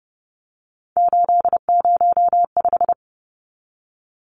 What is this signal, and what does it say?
Morse code, heard as a keyed tone, about 30 wpm, 700 Hz. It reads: 805
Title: Numbers - 3-Digit 30wpm